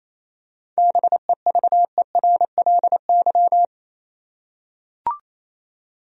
Morse code, heard as a keyed tone, about 28 wpm, 700 Hz. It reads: BEVERLY
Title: USA Common Names - 28wpm